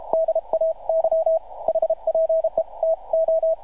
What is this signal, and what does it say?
Morse code, heard as a keyed tone, about 32 wpm, 650 Hz. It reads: DAY HPE TO
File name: archived / CRSnet